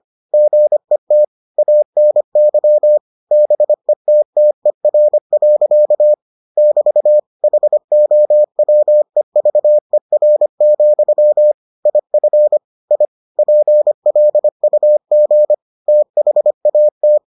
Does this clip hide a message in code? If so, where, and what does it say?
Yes: Morse code from 0.3 to 17.2 s, heard as a keyed tone, about 25 wpm, 600 Hz. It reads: GET ANY BETTER. <BT> HOWEVER, IF I PLUG THAT